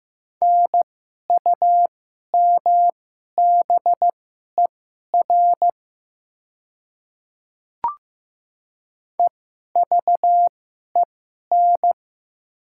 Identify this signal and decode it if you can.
Morse code, heard as a keyed tone, about 15 wpm, spaced as for 11 wpm, 700 Hz. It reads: NUMBER EVEN
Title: 2x Character Spacing - Top 200 Words - 15wpm